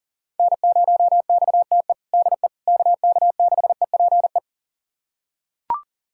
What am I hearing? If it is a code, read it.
Morse code, heard as a keyed tone, about 40 wpm, 700 Hz. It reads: N0XTE DE KK6EPE
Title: USA Callsign DE Callsign 40wpm